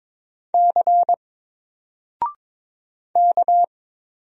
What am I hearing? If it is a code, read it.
Morse code, heard as a keyed tone, about 22 wpm, 700 Hz. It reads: C K
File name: Single-Letter-Number-22wpm